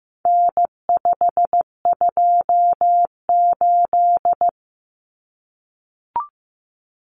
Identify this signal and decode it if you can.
Morse code, heard as a keyed tone, about 15 wpm, 700 Hz. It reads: N528
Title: Common Cut Numbers - 4-Digit 15wpm